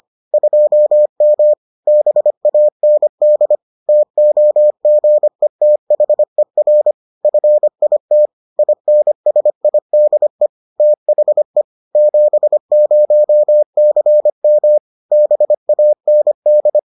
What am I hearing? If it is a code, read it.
Morse code, heard as a keyed tone, about 25 wpm, 600 Hz. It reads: 2M BAND TOGETHER FIT INSIDE THE 70CM BAND